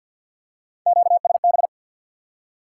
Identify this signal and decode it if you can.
Morse code, heard as a keyed tone, about 52 wpm, 700 Hz. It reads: QSB
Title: Top 180 QSO Elements 50wpm